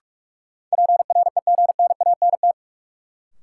Morse code, heard as a keyed tone, about 44 wpm, 700 Hz. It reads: PREGNANT
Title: Sentences from Top 1000 Words - Review Part 6 - New Word Drill - 45wpm